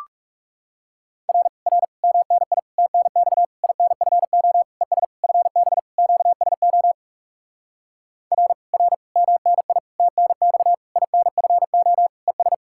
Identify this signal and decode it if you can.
Morse code, heard as a keyed tone, about 40 wpm, 700 Hz. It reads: R R MNI TNX INFO ES FB QSO R R MNI TNX INFO ES